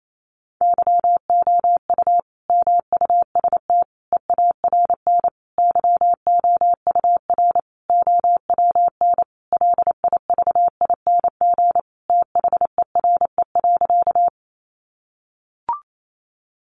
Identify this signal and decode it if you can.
Morse code, heard as a keyed tone, about 28 wpm, 700 Hz. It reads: YOU MUST EARN YOUR OWN LIVING THERE.
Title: Sentences from Top 1500 Words 28wpm